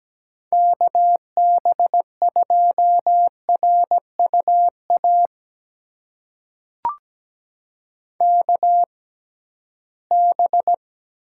Morse code, heard as a keyed tone, about 17 wpm, spaced as for 13 wpm, 700 Hz. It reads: KB2RUA KB
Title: USA Call Signs - 6 Characters - 6x Character Spacing - 17wpm